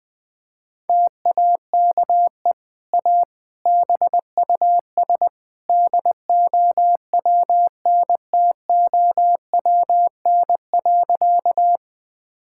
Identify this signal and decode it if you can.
Morse code, heard as a keyed tone, about 21 wpm, 700 Hz. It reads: TAKE A BUS DOWNTOWN.
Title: Easy Sentences with Hints - 20wpm